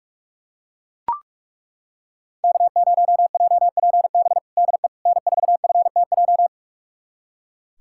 Morse code, heard as a keyed tone, about 41 wpm, 700 Hz. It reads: K0JPB DE N4FTJ
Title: USA Callsign DE Callsign - 5 Characters - 45wpm